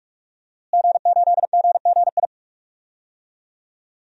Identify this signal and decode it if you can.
Morse code, heard as a keyed tone, about 46 wpm, 700 Hz. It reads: G8GGI
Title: DXCC-100 Call Signs - 5 Characters - Repeated 2x - 45wpm